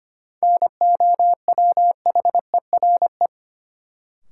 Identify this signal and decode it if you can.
Morse code, heard as a keyed tone, about 25 wpm, 700 Hz. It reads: NOWHERE
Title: Sets of 5 Words Encore - New Word Drill - 25wpm